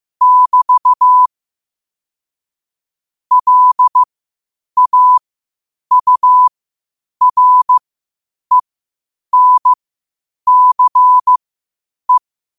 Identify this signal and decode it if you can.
Morse code, heard as a keyed tone, about 15 wpm, spaced as for 8 wpm, 1000 Hz. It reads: <BT> LAURENCE
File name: qotdfiles / 8wpm